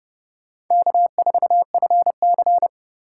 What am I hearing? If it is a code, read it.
Morse code, heard as a keyed tone, about 28 wpm, 700 Hz. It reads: K4FC